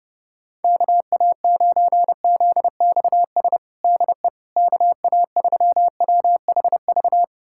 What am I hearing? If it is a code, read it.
Morse code, heard as a keyed tone, about 30 wpm, 700 Hz. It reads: KA9ZXS DE KA3WHV
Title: USA Callsign DE Callsign - 6 Characters - 30wpm